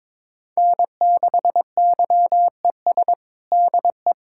Morse code, heard as a keyed tone, about 22 wpm, 700 Hz. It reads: N6YES DE
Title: K1USN SST QSO Practice 22wpm